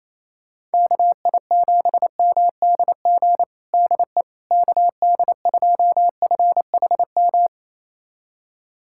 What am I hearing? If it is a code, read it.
Morse code, heard as a keyed tone, about 28 wpm, 700 Hz. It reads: KI7MDG DE KD2FHM
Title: USA Callsign DE Callsign 28wpm